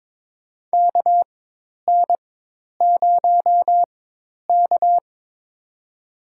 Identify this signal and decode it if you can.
Morse code, heard as a keyed tone, about 21 wpm, spaced as for 10 wpm, 700 Hz. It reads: KN0K
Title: USA Call Signs - 4x Character Spacing - 22wpm